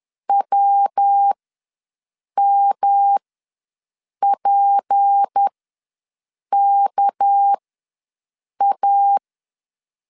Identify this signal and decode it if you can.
Morse code, heard as a keyed tone, about 10 wpm, spaced as for 6 wpm, 800 Hz. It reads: WMPKA
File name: LEZIONI / 6 PKX0 / 6 PK ws 10wpm 3745